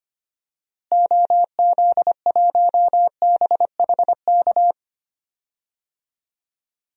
Morse code, heard as a keyed tone, about 25 wpm, 700 Hz. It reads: OZ1BHK
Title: DXCC-100 Call Signs - 6 Characters - Repeated 3x - 25wpm